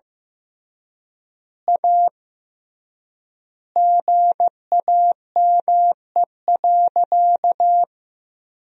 Morse code, heard as a keyed tone, about 15 wpm, 700 Hz. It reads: A GAME.
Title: 3x Word-Spacing - Sentences from Top 500 Words - 15wpm